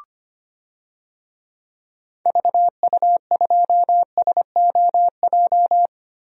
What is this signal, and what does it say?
Morse code, heard as a keyed tone, about 25 wpm, 700 Hz. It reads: VU2SOJ
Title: DXCC-100 Call Signs - 6 Characters - Repeated 2x - 25wpm